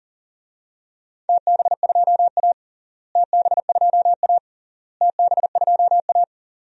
Morse code, heard as a keyed tone, about 40 wpm, 700 Hz. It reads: TB2A TB2A TB2A
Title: DXCC-100 Call Signs - 4 Characters - Repeated 3x - 40wpm